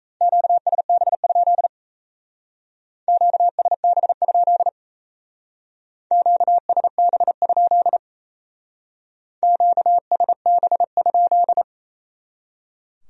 Morse code, heard as a keyed tone, about 34 wpm, 700 Hz. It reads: QSB? QSB? QSB? QSB?
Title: Speed-Racing Q-Codes - 28wpm